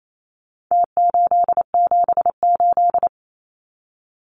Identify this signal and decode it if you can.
Morse code, heard as a keyed tone, about 28 wpm, 700 Hz. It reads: T878
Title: Numbers - Contest Style - 4-Digit 28wpm